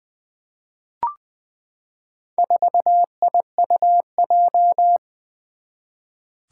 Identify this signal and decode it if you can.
Morse code, heard as a keyed tone, about 20 wpm, 700 Hz. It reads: 4IUJ
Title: Four Challenging Letter-Number Combinations 20wpm